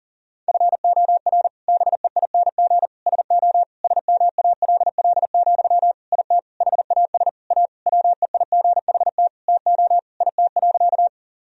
Morse code, heard as a keyed tone, about 40 wpm, 700 Hz. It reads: FOR BEING SO SMALL, IT HAS A WEIGHT TO IT.
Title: Sentences from Top 1000 Words - Review Part 3 - 40wpm